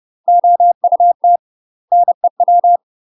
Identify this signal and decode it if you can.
Morse code, heard as a keyed tone, about 30 wpm, 700 Hz. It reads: OUT NEW
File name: UK MASTER QSOs TRAINING / 30WPM / UK MASTER QSOs 0007